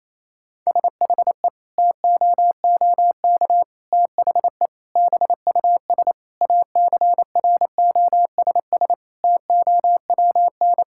Morse code, heard as a keyed tone, about 28 wpm, 690 Hz. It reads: SHE TOOK THE BUS ACROSS TOWN
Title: Sentences from Top 1000 Words - Review Part 6 - 28wpm